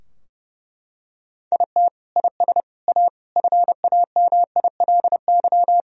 Morse code, heard as a keyed tone, about 30 wpm, 700 Hz. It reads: IT IS A FAMILY
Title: Sentences from Top 1000 Words - Review Part 2 - 30wpm